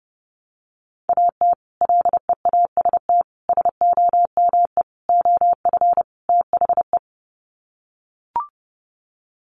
Morse code, heard as a keyed tone, about 28 wpm, 700 Hz. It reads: AT LEAST SOME OF THE
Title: Sets of 5 Words Encore 30wpm